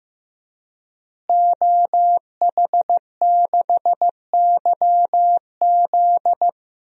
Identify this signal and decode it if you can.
Morse code, heard as a keyed tone, about 14 wpm, 700 Hz. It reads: OH6YZ
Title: DXCC-100 Call Signs - 5 Characters - Repeated 2x - 15wpm